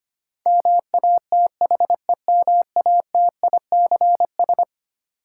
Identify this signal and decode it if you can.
Morse code, heard as a keyed tone, about 25 wpm, 700 Hz. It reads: MATHEMATICS
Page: morse-code-ninja-practice-25wpm